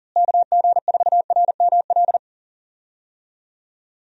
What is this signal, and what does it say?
Morse code, heard as a keyed tone, about 40 wpm, 700 Hz. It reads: KG4RML
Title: USA Call Signs - 6 Characters - Repeated 2x - 40wpm